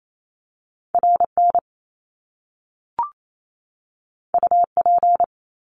Morse code, heard as a keyed tone, about 28 wpm, 700 Hz. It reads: R N U P
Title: Top Two Letter Combinations in Words 28wpm